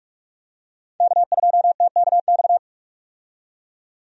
Morse code, heard as a keyed tone, about 46 wpm, 700 Hz. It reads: K2TKX
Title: USA Call Signs - 5 Characters - 45wpm